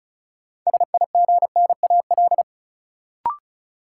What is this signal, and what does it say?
Morse code, heard as a keyed tone, about 34 wpm, 700 Hz. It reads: SIGNAL
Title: Sentences from Top 1400 Words - New Word Drill Alt - 35wpm